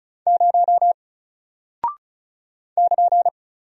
Morse code, heard as a keyed tone, about 36 wpm, 700 Hz. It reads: 0 <KN>
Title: Single Character (Letter:Number:Punctuation:Prosign) Alt 35wpm